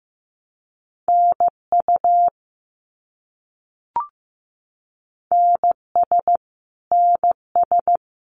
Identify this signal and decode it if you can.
Morse code, heard as a keyed tone, about 15 wpm, 700 Hz. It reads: NU NS NS
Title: Canadian Provinces and Territories - Repeated 3x - 15wpm